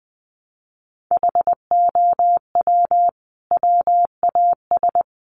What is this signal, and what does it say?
Morse code, heard as a keyed tone, about 20 wpm, 700 Hz. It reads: HOW WAS
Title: Sentences from Top 100 Words 20wpm